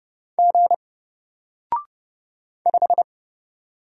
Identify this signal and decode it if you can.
Morse code, heard as a keyed tone, about 30 wpm, 700 Hz. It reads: G 5
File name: Single-Letter-Number-30wpm